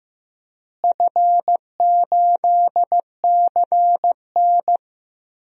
Morse code, heard as a keyed tone, about 16 wpm, 700 Hz. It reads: F8CN
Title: DXCC-100 Call Signs - 4 Characters - Repeated 3x - 15wpm